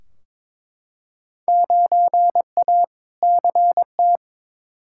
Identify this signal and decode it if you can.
Morse code, heard as a keyed tone, about 21 wpm, 700 Hz. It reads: 9A CT
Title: Field Day Exchanges Encore - 22wpm